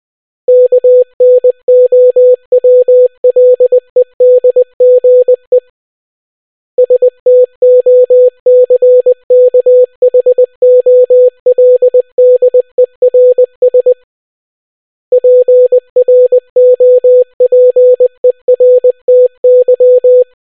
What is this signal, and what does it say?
Morse code, heard as a keyed tone, about 20 wpm, 500 Hz. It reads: KNOWLEDGE STOCKHOLDERS PROPERTY